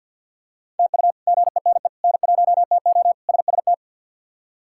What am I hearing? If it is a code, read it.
Morse code, heard as a keyed tone, about 51 wpm, 700 Hz. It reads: TU GENE N1TO SST
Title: K1USN SST QSO Practice - 50wpm